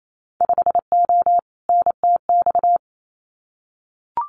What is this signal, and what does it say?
Morse code, heard as a keyed tone, about 28 wpm, 700 Hz. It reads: 5O NTX
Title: Winter Field Day Exchanges Encore 28wpm